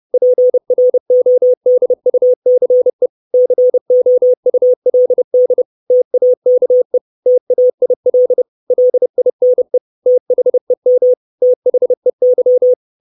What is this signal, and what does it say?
Morse code, heard as a keyed tone, about 30 wpm, 500 Hz. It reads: PRODUCE COULD TAKE TAIL LINE THEM THEY
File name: 632 Words 30WPM